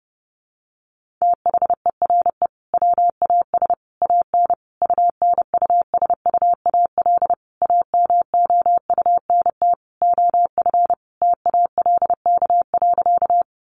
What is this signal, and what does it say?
Morse code, heard as a keyed tone, about 30 wpm, 700 Hz. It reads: THERE WAS AN UNUSUAL AMOUNT OF TALK.
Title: Sentences from Top 1400 Words 30wpm